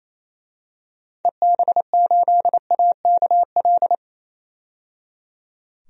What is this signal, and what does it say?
Morse code, heard as a keyed tone, about 28 wpm, 700 Hz. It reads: EB8AKL
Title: DXCC-100 Call Signs - 6 Characters - 28wpm